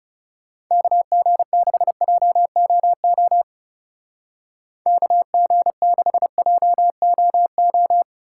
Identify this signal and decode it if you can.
Morse code, heard as a keyed tone, about 34 wpm, 700 Hz. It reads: KG6JOO KG6JOO
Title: Speed Racing - USA 2x3 Call Signs 30wpm